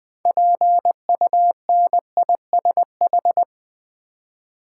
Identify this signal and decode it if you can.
Morse code, heard as a keyed tone, about 20 wpm, 700 Hz. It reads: PUNISH
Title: Build A Word - 3 Letters At A Time - 6-Letter Words - 20wpm